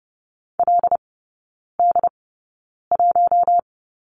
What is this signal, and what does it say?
Morse code, heard as a keyed tone, about 30 wpm, spaced as for 9 wpm, 700 Hz. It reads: LD1
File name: course / 30:10wpm / 087 Lesson 15e - Number 5 - Three Characters 30wpm